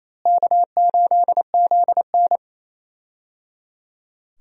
Morse code, heard as a keyed tone, about 28 wpm, 700 Hz. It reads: K8ZN
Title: USA Call Signs - 4 Characters - Repeated 3x - 28wpm